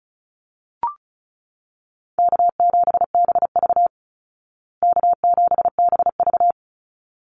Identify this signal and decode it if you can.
Morse code, heard as a keyed tone, about 34 wpm, 700 Hz. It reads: K7BV K7BV
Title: Easy Call Signs Repeated 2x - 35wpm